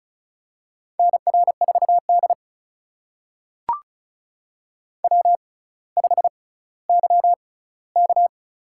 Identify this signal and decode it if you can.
Morse code, heard as a keyed tone, about 36 wpm, spaced as for 18 wpm, 700 Hz. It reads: NR4D W5YK
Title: USA Call Signs - 4 Characters - 6x Character Spacing - 35wpm